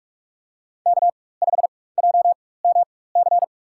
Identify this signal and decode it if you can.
Morse code, heard as a keyed tone, about 44 wpm, spaced as for 21 wpm, 700 Hz. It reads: K5JMC
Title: USA Call Signs - 5 Characters - 4x Character Spacing - 45wpm